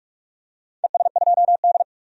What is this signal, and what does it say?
Morse code, heard as a keyed tone, about 44 wpm, 700 Hz. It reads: ES2D
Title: DXCC-100 Call Signs - 4 Characters - 45wpm